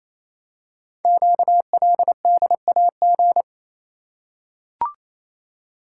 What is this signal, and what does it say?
Morse code, heard as a keyed tone, about 27 wpm, 700 Hz. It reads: QLDAG
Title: Code Groups - 5 Letters - 28wpm